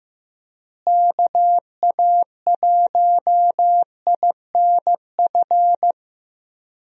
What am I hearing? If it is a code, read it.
Morse code, heard as a keyed tone, about 15 wpm, 700 Hz. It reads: KA1INF
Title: USA Call Signs - 6 Characters - Repeated 3x - 15wpm